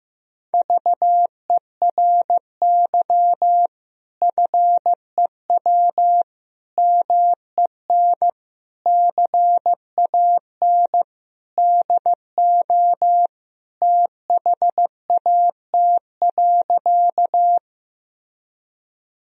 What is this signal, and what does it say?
Morse code, heard as a keyed tone, about 15 wpm, 700 Hz. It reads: VERY FEW MEN CAN DO THAT.
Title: Sentences from Top 1000 Words - Review Part 1 - 15wpm